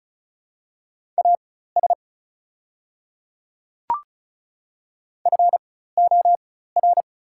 Morse code, heard as a keyed tone, about 34 wpm, spaced as for 16 wpm, 700 Hz. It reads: AS FOR
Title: 4x Character Spacing - Top 100 Words - 35wpm